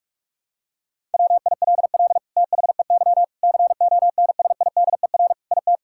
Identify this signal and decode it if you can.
Morse code, heard as a keyed tone, about 46 wpm, 700 Hz. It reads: WILL THEY CONSIDER IT
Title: Sentences from Top 1000 Words - Review Part 5 - 45wpm